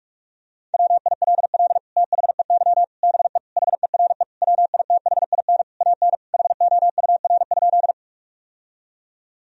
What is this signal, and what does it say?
Morse code, heard as a keyed tone, about 43 wpm, 700 Hz. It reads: WILL THEY BE HERE WITHIN AN HOUR?
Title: Sentences from Top 1000 Words - Review Part 4 - 45wpm